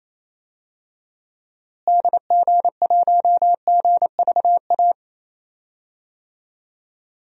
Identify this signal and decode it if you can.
Morse code, heard as a keyed tone, about 28 wpm, 700 Hz. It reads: DG1GVA
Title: DXCC-100 Call Signs - 6 Characters - Repeated 2x - 28wpm